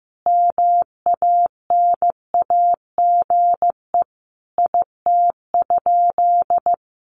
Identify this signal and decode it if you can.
Morse code, heard as a keyed tone, about 15 wpm, 700 Hz. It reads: MANAGE IT?
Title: Sentences from Top 1400 Words 15wpm